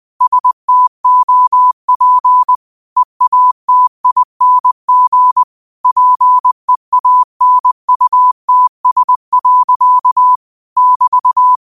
Morse code, heard as a keyed tone, about 20 wpm, 1000 Hz. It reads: STOP EATING PEANUTS. <BT>